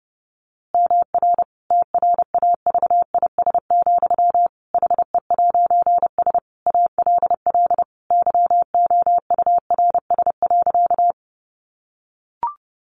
Morse code, heard as a keyed tone, about 30 wpm, 700 Hz. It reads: MR TRAVIS, HE'S ALL YOURS.
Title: A Sound Of Thunder 30wpm